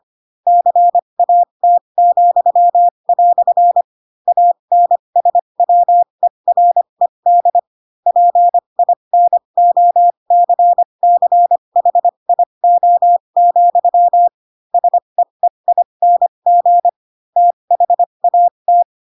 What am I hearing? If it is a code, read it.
Morse code, heard as a keyed tone, about 25 wpm, spaced as for 22 wpm, 700 Hz. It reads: CAT," ANSWERED PINOCCHIO, SEEING THAT
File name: Pinocchio / Pinocchio ch7 18 25 ik1hge